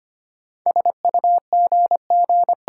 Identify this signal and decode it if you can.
Morse code, heard as a keyed tone, about 25 wpm, 700 Hz. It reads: SUGG
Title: Sentences from Top 1300 Words - New Word Drill Alt - 25wpm